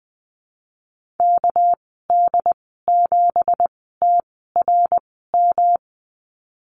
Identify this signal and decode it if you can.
Morse code, heard as a keyed tone, about 20 wpm, spaced as for 15 wpm, 700 Hz. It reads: KD7TRM
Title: 200 Lesson 32g - Letter k - Call Signs 20wpm